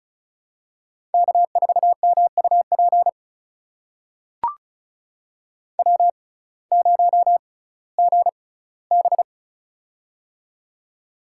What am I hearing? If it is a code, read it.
Morse code, heard as a keyed tone, about 36 wpm, spaced as for 19 wpm, 700 Hz. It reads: K4MUP W0GB
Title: USA Call Signs - 6x Character Spacing - 35wpm